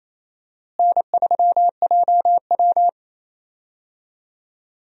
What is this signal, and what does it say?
Morse code, heard as a keyed tone, about 28 wpm, 700 Hz. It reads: N3JW
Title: USA Call Signs - 4 Characters - Repeated 3x - 28wpm